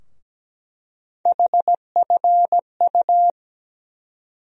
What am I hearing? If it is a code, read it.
Morse code, heard as a keyed tone, about 17 wpm, 700 Hz. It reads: HFU
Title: USA Call Sign Postfixes - 3 Letters 17wpm